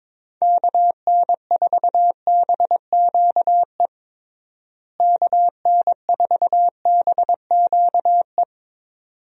Morse code, heard as a keyed tone, about 21 wpm, 700 Hz. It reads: KN4BQE KN4BQE
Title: USA Call Signs - 6 Characters - Repeated 2x - 22wpm